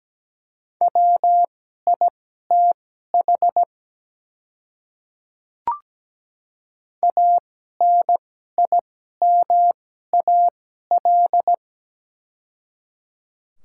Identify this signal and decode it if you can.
Morse code, heard as a keyed tone, about 17 wpm, spaced as for 12 wpm, 700 Hz. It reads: WITH ANIMAL
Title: 2x Character Spacing - Top 200 Words - 17wpm